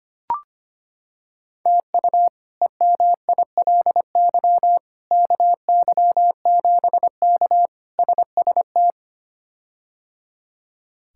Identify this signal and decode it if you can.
Morse code, heard as a keyed tone, about 25 wpm, 700 Hz. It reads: TU EMILY KY7K SST
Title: K1USN SST QSO Practice - 25wpm